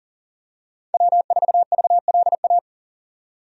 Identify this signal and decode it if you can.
Morse code, heard as a keyed tone, about 40 wpm, 700 Hz. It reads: W4VLA
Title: USA Call Signs - 5 Characters - 40wpm